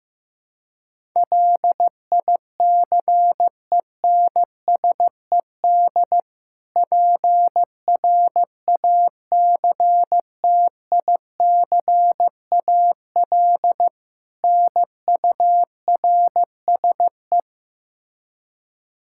Morse code, heard as a keyed tone, about 15 wpm, 700 Hz. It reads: LICENSED PRACTICAL NURSE
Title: Practice QSO - Exchange 3 - More Info - 15wpm